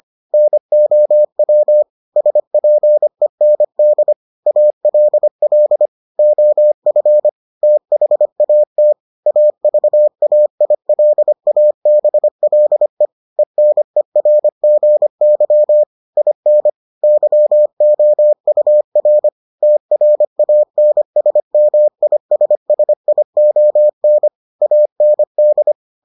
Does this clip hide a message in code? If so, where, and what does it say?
Yes: Morse code from 0.3 to 25.7 s, heard as a keyed tone, about 25 wpm, 600 Hz. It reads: NOW SPEND ALL OF THAT AVAILABLE ENERGY IN YOUR TRANSMISSION AND